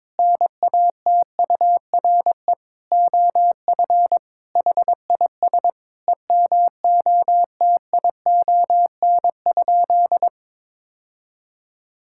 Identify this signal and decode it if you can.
Morse code, heard as a keyed tone, about 22 wpm, 700 Hz. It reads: NATURE OF HIS EMOTION?
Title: Sentences from Top 1600 Words 22wpm